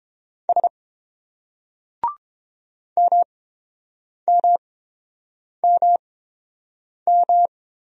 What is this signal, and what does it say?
Morse code, heard as a keyed tone, about 29 wpm, spaced as for 7 wpm, 700 Hz. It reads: S MMMM